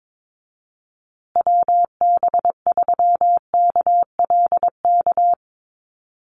Morse code, heard as a keyed tone, about 22 wpm, 700 Hz. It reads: WB3KLK
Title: USA Call Signs Repeated 3x 22wpm